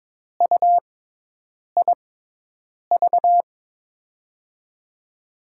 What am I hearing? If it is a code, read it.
Morse code, heard as a keyed tone, about 22 wpm, spaced as for 8 wpm, 700 Hz. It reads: UIV
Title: USA Call Signs - 5 Characters - 6x Character Spacing - 22wpm